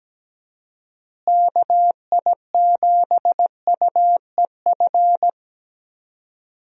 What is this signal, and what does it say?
Morse code, heard as a keyed tone, about 17 wpm, 700 Hz. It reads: KI7UEF
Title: USA Call Signs - 6 Characters - Repeated 3x - 17wpm